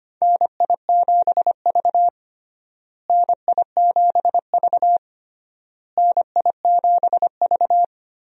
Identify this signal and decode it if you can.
Morse code, heard as a keyed tone, about 25 wpm, 700 Hz. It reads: NI7V NI7V NI7V
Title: USA Call Signs - 4 Characters - Repeated 3x - 25wpm